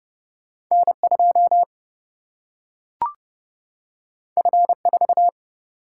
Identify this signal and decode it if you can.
Morse code, heard as a keyed tone, about 30 wpm, 700 Hz. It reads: N 2 F 4
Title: Single-Letter Single-Number 30wpm